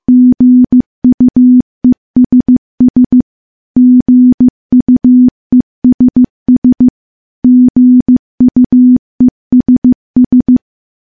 Morse code, heard as a keyed tone, about 15 wpm, 250 Hz. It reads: GUESS GUESS GUESS
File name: morse-training-15WPM